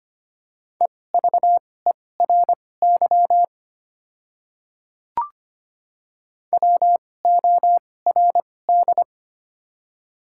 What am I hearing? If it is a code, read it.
Morse code, heard as a keyed tone, about 25 wpm, spaced as for 18 wpm, 700 Hz. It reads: EVERY WORD
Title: 2x Character Spacing - Top 300 Words - 25wpm